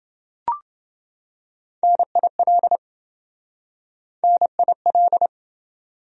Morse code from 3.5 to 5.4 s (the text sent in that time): NIL